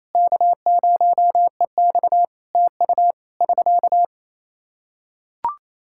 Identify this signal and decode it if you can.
Morse code, heard as a keyed tone, about 28 wpm, 700 Hz. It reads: K0EX TU <SK>
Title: Practice QSO - Exchange 4 - Ending - 28wpm